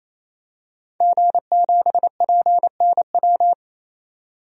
Morse code, heard as a keyed tone, about 28 wpm, 700 Hz. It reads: G7PNW
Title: DXCC-100 Call Signs - 5 Characters - 28wpm